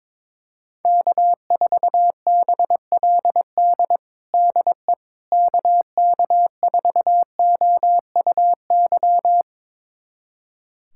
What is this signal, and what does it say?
Morse code, heard as a keyed tone, about 22 wpm, 700 Hz. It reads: K4BLD DE KK4OUY
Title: USA Callsign DE Callsign 22wpm